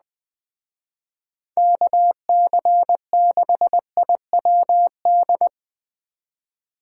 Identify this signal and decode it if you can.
Morse code, heard as a keyed tone, about 21 wpm, 700 Hz. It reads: KC6IWD